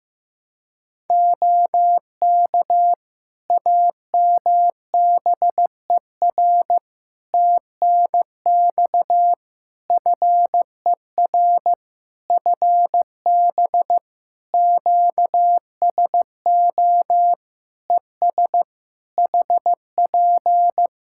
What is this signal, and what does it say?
Morse code, heard as a keyed tone, about 15 wpm, 700 Hz. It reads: OK AMBER TNX FER FB QSO ES HP
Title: Practice QSO - Exchange 4 - Ending - 15wpm